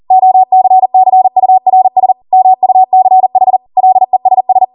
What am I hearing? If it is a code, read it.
Morse code, heard as a keyed tone, about 40 wpm, 750 Hz. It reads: OCCURS MUCH LESS